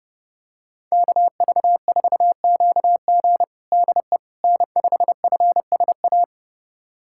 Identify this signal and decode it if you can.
Morse code, heard as a keyed tone, about 30 wpm, 700 Hz. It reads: KV4QG DE N5FSA
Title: USA Callsign DE Callsign - 5 Characters - 30wpm